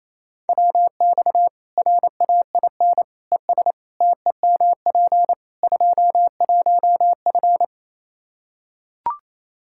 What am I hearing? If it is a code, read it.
Morse code, heard as a keyed tone, about 28 wpm, 700 Hz. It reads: WX RAIN ES TEMP 21F
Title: Practice QSO - Exchange 2 - Info - 28wpm